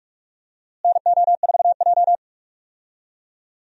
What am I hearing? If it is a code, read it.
Morse code, heard as a keyed tone, about 45 wpm, 700 Hz. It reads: NO4J
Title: USA Call Signs - 4 Characters - Repeated 3x - 45wpm